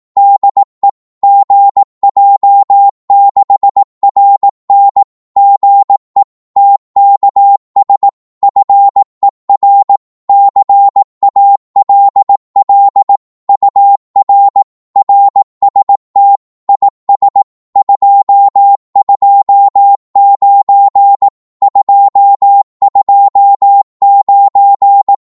Morse code, heard as a keyed tone, about 18 wpm, 800 Hz. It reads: DE GJ6RN GE TKS FER CALL UR RST IS 229 229
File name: Morse / qso-18wpm